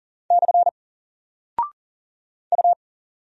Morse code, heard as a keyed tone, about 40 wpm, 700 Hz. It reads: / U
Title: Single Character (Letter:Number:Punctuation:Prosign) 40wpm